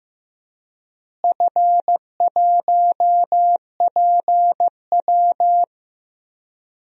Morse code, heard as a keyed tone, about 15 wpm, 700 Hz. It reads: F1PW
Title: DXCC-100 Call Signs - 4 Characters - Repeated 2x - 15wpm